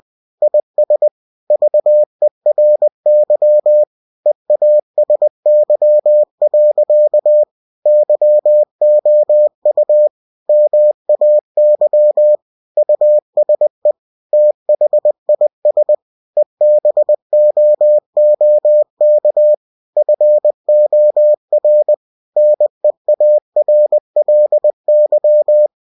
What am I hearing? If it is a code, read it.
Morse code, heard as a keyed tone, about 20 wpm, 600 Hz. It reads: IS VERY EASY. YOU MAY USE THIS EBOOK FOR NEARLY
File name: cw-books / winnie-the-pooh / winnie-20wpm-page-0112